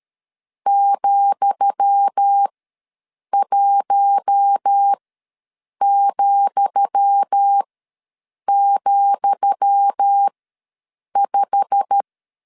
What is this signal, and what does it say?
Morse code, heard as a keyed tone, about 13 wpm, spaced as for 7 wpm, 800 Hz. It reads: ,1,,5
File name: LEZIONI / 10 SIGNS / 10D COMA ws 12wpm 3618